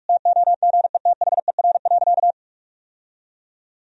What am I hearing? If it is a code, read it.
Morse code, heard as a keyed tone, about 41 wpm, 700 Hz. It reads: TOGETHER.
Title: Sentences from Top 1000 Words - Review Part 3 - 45wpm